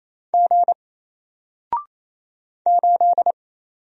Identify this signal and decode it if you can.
Morse code, heard as a keyed tone, about 28 wpm, 700 Hz. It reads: G 8
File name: Single-Letter-Number-28wpm